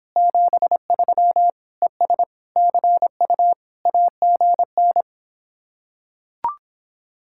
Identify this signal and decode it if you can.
Morse code, heard as a keyed tone, about 26 wpm, 700 Hz. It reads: 73 ES CU AGN